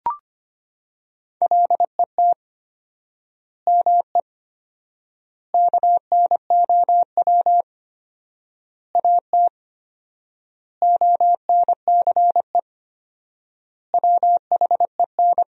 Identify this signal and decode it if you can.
Morse code, heard as a keyed tone, about 25 wpm, 700 Hz. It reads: LET ME KNOW AT ONCE WHEN
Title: 4x Word-Spacing - Sentences from Top 300 Words - 25wpm